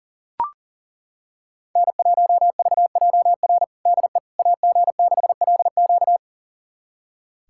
Morse code, heard as a keyed tone, about 40 wpm, 700 Hz. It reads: N1VJR DE AG6LQ
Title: USA Callsign DE Callsign - 5 Characters - 40wpm